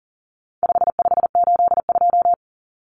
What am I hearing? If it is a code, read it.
Morse code, heard as a keyed tone, about 40 wpm, 700 Hz. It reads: 5582
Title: Numbers - 4-Digit 40wpm